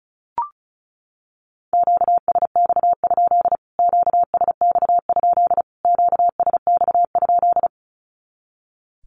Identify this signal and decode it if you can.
Morse code, heard as a keyed tone, about 34 wpm, 700 Hz. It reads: QSX? QSX? QSX?
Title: Q-Codes Repeated 3x 35wpm